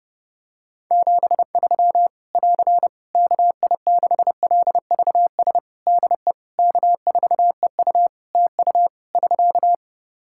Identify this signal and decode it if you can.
Morse code, heard as a keyed tone, about 30 wpm, 700 Hz. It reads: 73 <AR> KI6LVS DE K4EU TU <SK>
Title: K1USN SST QSO Practice 30wpm